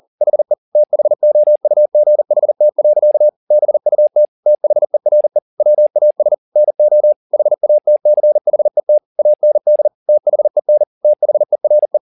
Morse code, heard as a keyed tone, about 40 wpm, 600 Hz. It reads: HE THOUGHT. BUT THERE WAS NO HATCHET AND THEN THERE
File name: the old man and the sea 2 40wpm